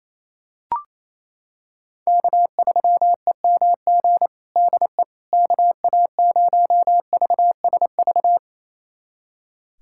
Morse code, heard as a keyed tone, about 28 wpm, 700 Hz. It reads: K3EMG DE KA0VSV
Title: USA Callsign DE Callsign 28wpm